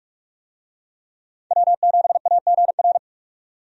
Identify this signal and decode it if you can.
Morse code, heard as a keyed tone, about 45 wpm, 700 Hz. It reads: W7AGR
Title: USA Call Signs - 5 Characters - 45wpm